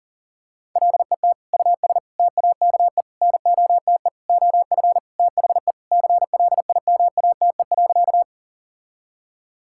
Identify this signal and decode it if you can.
Morse code, heard as a keyed tone, about 40 wpm, 700 Hz. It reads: LET US TAKE NOTE OF THE CLIMATE.
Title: Sentences from Top 1600 Words 40wpm